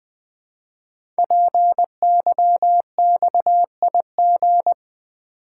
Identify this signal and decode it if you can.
Morse code, heard as a keyed tone, about 20 wpm, 700 Hz. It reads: PYXIG
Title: Code Groups - 5 Letters - 20wpm